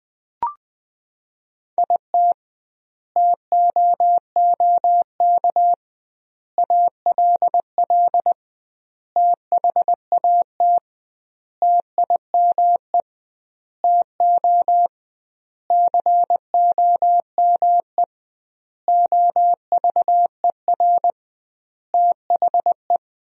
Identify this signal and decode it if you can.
Morse code, heard as a keyed tone, about 20 wpm, 700 Hz. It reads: IT TOOK ALL THAT TIME TO COME OVER THE
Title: 2x Word-Spacing - Sentences from Top 300 Words - 20wpm